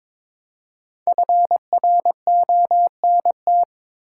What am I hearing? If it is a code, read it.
Morse code, heard as a keyed tone, about 22 wpm, 700 Hz. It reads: FRONT
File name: 5 Letter Words - First 3 Letters Given - 22wpm